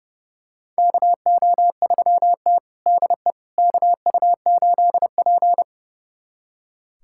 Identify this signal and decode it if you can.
Morse code, heard as a keyed tone, about 30 wpm, 700 Hz. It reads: KO3T DE KU8P
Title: USA Callsign DE Callsign - 4 Characters - 30wpm